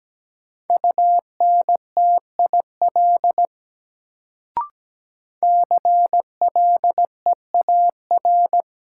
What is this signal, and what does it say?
Morse code, heard as a keyed tone, about 17 wpm, 700 Hz. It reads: UNTIL CLEAR
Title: Sentences from Top 500 Words - New Word Drill - 17wpm